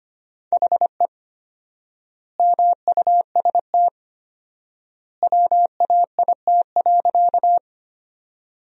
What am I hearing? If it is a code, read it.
Morse code, heard as a keyed tone, about 25 wpm, 700 Hz. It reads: HE MUST WAIT.
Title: 4x Word-Spacing - Sentences from Top 500 Words - 25wpm